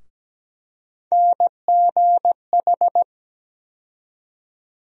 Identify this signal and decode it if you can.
Morse code, heard as a keyed tone, about 17 wpm, 700 Hz. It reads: NGH
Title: USA Call Sign Postfixes - 3 Letters 17wpm